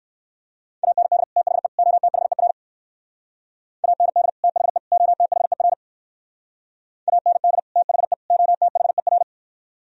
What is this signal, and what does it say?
Morse code, heard as a keyed tone, about 60 wpm, 700 Hz. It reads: AND THE OTHER AND THE OTHER AND THE OTHER